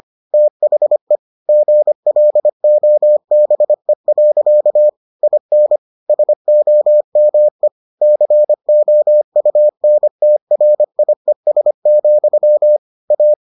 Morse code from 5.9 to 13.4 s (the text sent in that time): SOME COUNTRIES, A